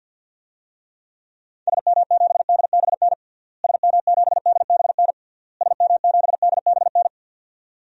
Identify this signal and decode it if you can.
Morse code, heard as a keyed tone, about 52 wpm, 700 Hz. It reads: SM7DBN SM7DBN SM7DBN
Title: DXCC-100 Call Signs - 6 Characters - Repeated 3x - 50wpm